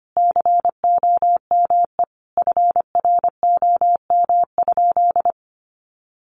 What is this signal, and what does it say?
Morse code, heard as a keyed tone, about 25 wpm, 700 Hz. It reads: COME FROM?
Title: Sentences from Top 1100 Words 25wpm